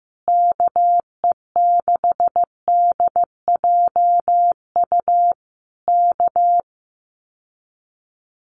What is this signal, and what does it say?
Morse code, heard as a keyed tone, about 15 wpm, 700 Hz. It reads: KE6DJU K
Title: Common QSO Phrases Encore 15wpm